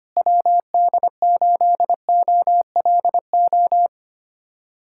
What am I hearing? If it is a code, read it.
Morse code, heard as a keyed tone, about 25 wpm, 700 Hz. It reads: WD8OLO
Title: USA Callsign DE Callsign - 6 Characters - 25wpm